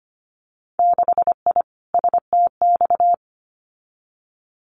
Morse code, heard as a keyed tone, about 25 wpm, 690 Hz. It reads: 6I STX
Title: Winter Field Day Exchanges Encore 25wpm